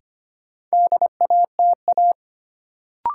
Morse code, heard as a keyed tone, about 25 wpm, 700 Hz. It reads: DATA
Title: Sentences from Top 1000 Words - Review Part 3 - New Word Drill - 25wpm